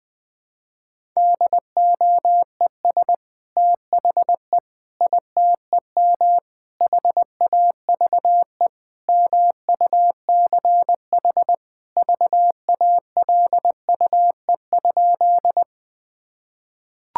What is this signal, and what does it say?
Morse code, heard as a keyed tone, about 20 wpm, 700 Hz. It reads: DOES THE ITEM HAVE MUCH VALUE?
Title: Sentences from Top 1000 Words - Review Part 6 - 20wpm